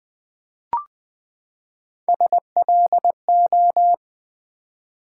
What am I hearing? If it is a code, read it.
Morse code, heard as a keyed tone, about 20 wpm, 700 Hz. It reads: SLO
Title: Build A Word - 3 Letters At A Time - 6-Letter Words - 20wpm